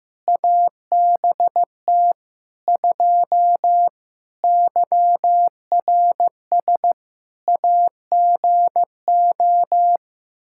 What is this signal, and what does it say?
Morse code, heard as a keyed tone, about 15 wpm, 700 Hz. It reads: ABT 2 YRS AGO
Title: Practice QSO - Exchange 3 - More Info - 15wpm